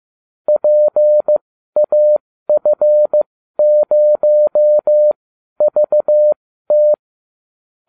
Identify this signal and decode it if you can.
Morse code, heard as a keyed tone, about 15 wpm, spaced as for 12 wpm, 600 Hz. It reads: PAF0VT